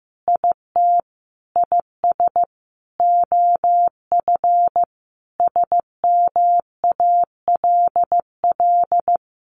Morse code, heard as a keyed tone, about 15 wpm, 700 Hz. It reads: IT IS OF SMALL
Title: Sentences from Top 1200 Words 15wpm